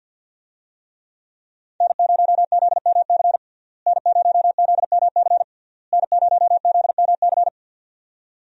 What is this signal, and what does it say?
Morse code, heard as a keyed tone, about 50 wpm, 700 Hz. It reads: N0ZMC N0ZMC N0ZMC
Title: USA Call Signs - 5 Characters - Repeated 3x - 50wpm